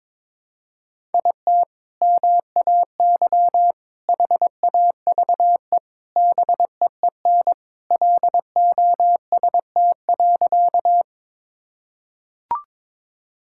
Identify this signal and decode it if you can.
Morse code, heard as a keyed tone, about 22 wpm, 700 Hz. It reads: IT MAY HAVE BEEN LOST.
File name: practice / 22wpm / Sentences with 3-Word Phrases - 22wpm